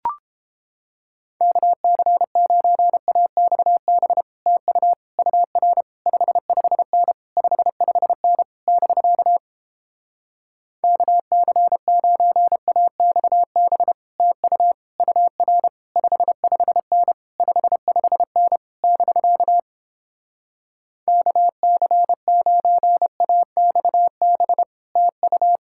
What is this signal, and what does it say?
Morse code, heard as a keyed tone, about 31 wpm, 700 Hz. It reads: KC9AXB TU UR 55N 55N <BK> KC9AXB TU UR 55N 55N <BK> KC9AXB TU